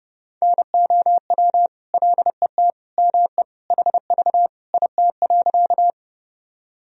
Morse code, heard as a keyed tone, about 30 wpm, 700 Hz. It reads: NOW LET ME HV IT.
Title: Sentences from Top 300 Words and Abbreviations 30wpm